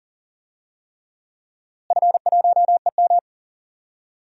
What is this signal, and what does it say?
Morse code, heard as a keyed tone, about 39 wpm, 700 Hz. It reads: F1EM
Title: DXCC-100 Call Signs - 4 Characters - Repeated 3x - 40wpm